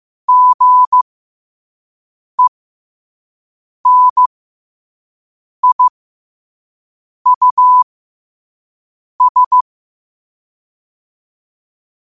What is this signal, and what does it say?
Morse code, heard as a keyed tone, about 15 wpm, spaced as for 5 wpm, 1000 Hz. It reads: GENIUS